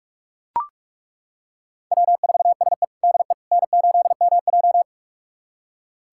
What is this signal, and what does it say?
Morse code, heard as a keyed tone, about 42 wpm, 700 Hz. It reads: W4SE DE N8MJ
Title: USA Callsign DE Callsign - 4 Characters - 45wpm